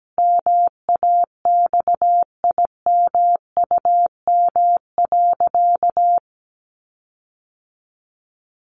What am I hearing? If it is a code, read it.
Morse code, heard as a keyed tone, about 17 wpm, 700 Hz. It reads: MAXIMUM.
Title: Sentences from Top 1300 Words 17wpm